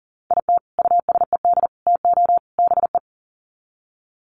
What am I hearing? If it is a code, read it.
Morse code, heard as a keyed tone, about 40 wpm, 700 Hz. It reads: IT USED TO BE
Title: Sets of 4 Words 40wpm